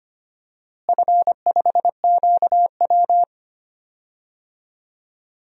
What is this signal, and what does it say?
Morse code, heard as a keyed tone, about 25 wpm, 700 Hz. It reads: F5QW
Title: DXCC-100 Call Signs - 4 Characters - Repeated 3x - 25wpm